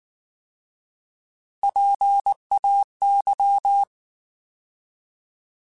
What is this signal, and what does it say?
Morse code, heard as a keyed tone, about 19 wpm, 800 Hz. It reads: PAY
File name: Alphabetical / Alphabetical mp3s / Ps @18